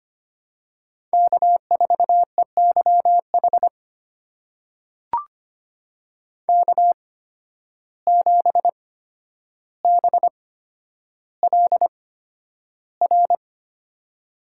Morse code, heard as a keyed tone, about 25 wpm, spaced as for 11 wpm, 700 Hz. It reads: K4EYH K7BLR
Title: USA Call Signs - 5 Characters - 8x Character Spacing - 25wpm